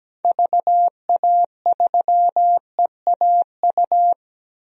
Morse code, heard as a keyed tone, about 17 wpm, 700 Hz. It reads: VA3EAU
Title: DXCC-100 Call Signs - 6 Characters - 17wpm